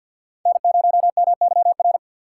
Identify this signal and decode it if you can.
Morse code, heard as a keyed tone, about 62 wpm, 700 Hz. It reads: N0MYR
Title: Speed Racing - USA 1x3 Call Signs 50wpm